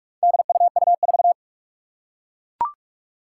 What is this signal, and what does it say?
Morse code, heard as a keyed tone, about 44 wpm, 700 Hz. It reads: DUU4
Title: Four Challenging Letter-Number Combinations 45wpm